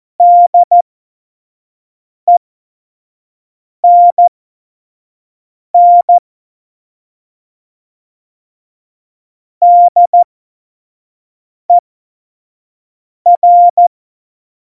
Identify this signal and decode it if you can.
Morse code, heard as a keyed tone, about 14 wpm, spaced as for 5 wpm, 700 Hz. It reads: DENN DER
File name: Morsekurs / Witze / Witz 29. September 2020, 25 F70